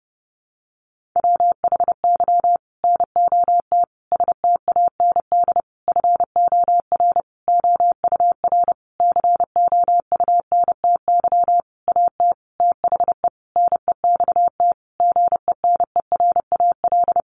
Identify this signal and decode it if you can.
Morse code, heard as a keyed tone, about 30 wpm, 700 Hz. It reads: WHY NOT STAND FOR OUR COUNTY AT THE NEXT GENERAL
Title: Sentences from Top 1500 Words 30wpm